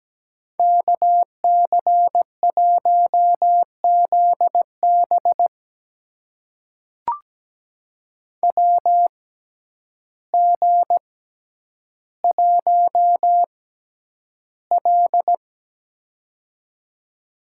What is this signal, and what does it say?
Morse code, heard as a keyed tone, about 17 wpm, spaced as for 9 wpm, 700 Hz. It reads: KC1ZB WG1L
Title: USA Call Signs - 6x Character Spacing - 17wpm